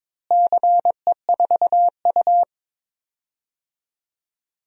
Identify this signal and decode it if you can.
Morse code, heard as a keyed tone, about 22 wpm, 700 Hz. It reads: CE4U
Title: DXCC-100 Call Signs - 4 Characters - Repeated 2x - 22wpm